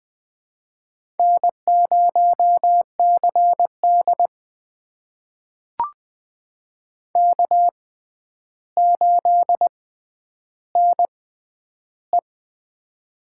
Morse code, heard as a keyed tone, about 20 wpm, spaced as for 10 wpm, 700 Hz. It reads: N0CD K8NE
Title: USA Call Signs - 4 Characters - 6x Character Spacing - 20wpm